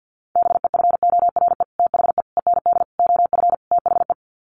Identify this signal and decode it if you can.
Morse code, heard as a keyed tone, about 50 wpm, 700 Hz. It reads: BEFORE THE END OF THE
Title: Sets of 5 Words Encore 50wpm